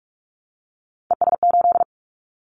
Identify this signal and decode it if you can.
Morse code, heard as a keyed tone, about 46 wpm, 700 Hz. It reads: ES8
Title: Callsign Prefixes for Top 50 DXCC - 45wpm